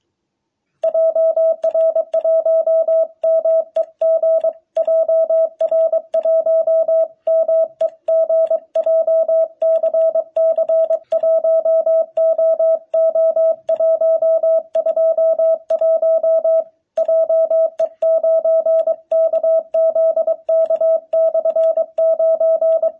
Morse code, heard as a keyed tone, about 22 wpm, 650 Hz. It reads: JR1MEG JR1MEG JCC1OO121 JE9KZK/9
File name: jh8cbh / 2021cw